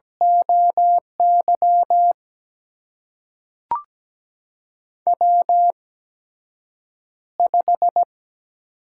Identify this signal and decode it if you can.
Morse code, heard as a keyed tone, about 17 wpm, spaced as for 7 wpm, 700 Hz. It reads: OY W5
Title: USA Call Signs - 4 Characters - 8x Character Spacing - 17wpm